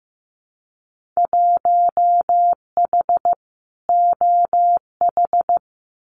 Morse code, heard as a keyed tone, about 15 wpm, 700 Hz. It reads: 1H OH
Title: Winter Field Day Exchanges 15wpm